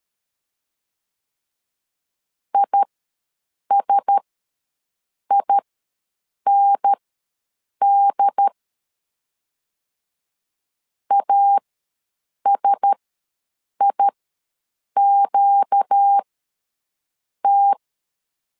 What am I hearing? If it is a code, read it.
Morse code, heard as a keyed tone, about 13 wpm, spaced as for 7 wpm, 800 Hz. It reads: ISIND ASIQT
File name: LEZIONI / 4 Q1S ws 12wpm 1827